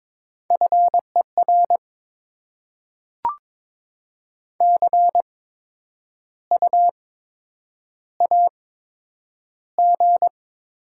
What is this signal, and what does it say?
Morse code, heard as a keyed tone, about 22 wpm, spaced as for 9 wpm, 700 Hz. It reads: FER CU AG
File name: Top 20 QSO Elements - 8x Character Spacing - 22wpm